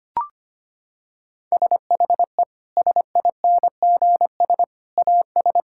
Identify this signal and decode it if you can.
Morse code, heard as a keyed tone, about 25 wpm, 700 Hz. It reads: SHE SINGS AS
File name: practice / 25wpm / Sentences with 3-Word Phrases - 25wpm